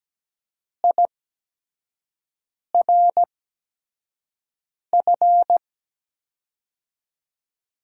Morse code, heard as a keyed tone, about 17 wpm, spaced as for 5 wpm, 700 Hz. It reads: IRF
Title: USA Call Signs - 8x Character Spacing - 17wpm